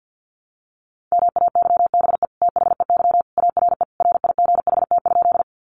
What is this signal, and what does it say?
Morse code, heard as a keyed tone, about 50 wpm, 700 Hz. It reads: MAYBE THEY ARE RIGHT?
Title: Sentences from Top 1000 Words 50wpm